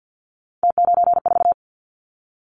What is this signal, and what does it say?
Morse code, heard as a keyed tone, about 50 wpm, 700 Hz. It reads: T94
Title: Callsign Prefixes for Top 100 DXCC - 50wpm